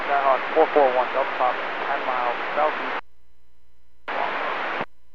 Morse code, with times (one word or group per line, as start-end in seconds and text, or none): none